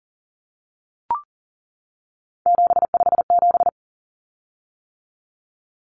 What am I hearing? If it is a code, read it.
Morse code, heard as a keyed tone, about 40 wpm, 700 Hz. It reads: 757
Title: Common Cut Numbers - 3-Digit 40wpm